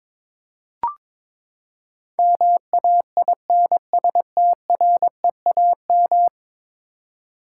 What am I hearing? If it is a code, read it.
Morse code, heard as a keyed tone, about 22 wpm, 700 Hz. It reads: MAINSTREAM
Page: morse-code-ninja-practice-22wpm